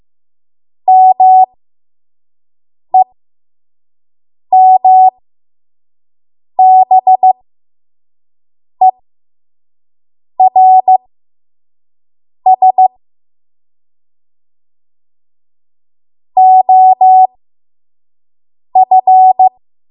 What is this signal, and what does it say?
Morse code, heard as a keyed tone, about 15 wpm, spaced as for 5 wpm, 750 Hz. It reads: MEMBERS OF
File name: Morse / 05WPM